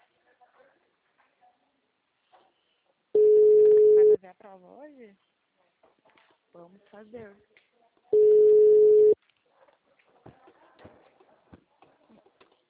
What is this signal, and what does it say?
Ringback tone (Europe, 425 Hz), with speech.